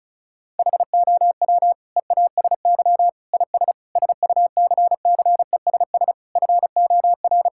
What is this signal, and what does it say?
Morse code, heard as a keyed tone, about 36 wpm, 700 Hz. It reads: HOW EASY IS SUCCESS FOR
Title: Sentences from Top 1000 Words - Review Part 6 - 35wpm